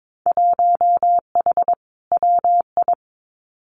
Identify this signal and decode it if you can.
Morse code, heard as a keyed tone, about 22 wpm, 700 Hz. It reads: 1H WI
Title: Winter Field Day Exchanges 22wpm